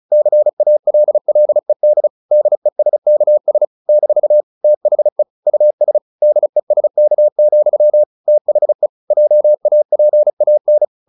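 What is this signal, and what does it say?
Morse code, heard as a keyed tone, about 34 wpm, 600 Hz. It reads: CALLED DESKS - THE US DESK, THE JAPAN
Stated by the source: Morse practice at 35 wpm from CW med Gnister, sending Ken Follett's novel Never in English.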